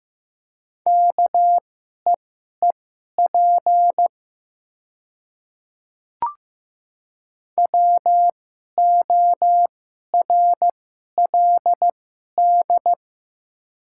Morse code, heard as a keyed tone, about 15 wpm, spaced as for 11 wpm, 700 Hz. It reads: KEEP WORLD
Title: 2x Character Spacing - Top 500 Words - 15wpm